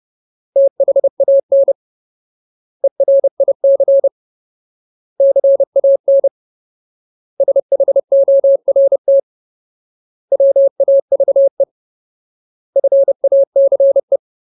Morse code, heard as a keyed tone, about 30 wpm, 550 Hz. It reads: THAN ERIC CAN SHORT WAVE FACE
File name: Ultimate Word List / 30WPM Ultimate